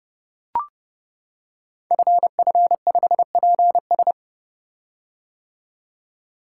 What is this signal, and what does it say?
Morse code, heard as a keyed tone, about 30 wpm, 700 Hz. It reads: FF5PS
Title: Code Groups - 5 Letter:Numbers - 30wpm